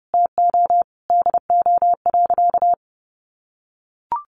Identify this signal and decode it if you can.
Morse code, heard as a keyed tone, about 28 wpm, 700 Hz. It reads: TO DO.
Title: Sentences from Top 200 Words 30wpm